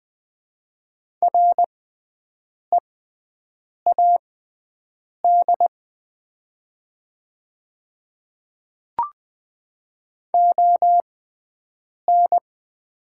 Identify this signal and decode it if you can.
Morse code, heard as a keyed tone, about 20 wpm, spaced as for 7 wpm, 700 Hz. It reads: READ ON
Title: 6x Character Spacing - Top 500 Words - 20wpm